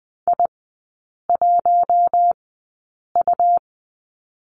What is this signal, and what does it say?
Morse code, heard as a keyed tone, about 20 wpm, spaced as for 8 wpm, 700 Hz. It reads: I1U
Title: 218 Lesson 35e - Number 0 - Three Characters 20wpm